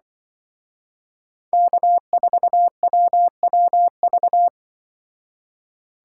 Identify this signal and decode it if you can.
Morse code, heard as a keyed tone, about 24 wpm, 700 Hz. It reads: K4WWV
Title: Speed Racing - USA 1x3 Call Signs 20wpm